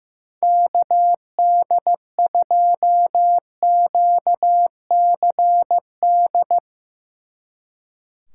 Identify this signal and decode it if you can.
Morse code, heard as a keyed tone, about 15 wpm, 700 Hz. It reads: KD2QCD
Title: Speed Racing - USA 2x3 Call Signs 15wpm